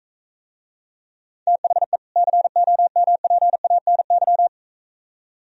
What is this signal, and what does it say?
Morse code, heard as a keyed tone, about 45 wpm, 700 Hz. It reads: THE COMPANY